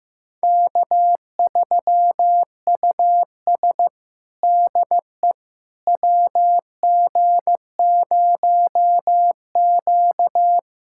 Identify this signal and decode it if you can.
Morse code, heard as a keyed tone, about 15 wpm, 700 Hz. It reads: K3US DE WG0Q
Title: USA Callsign DE Callsign - 4 Characters - 15wpm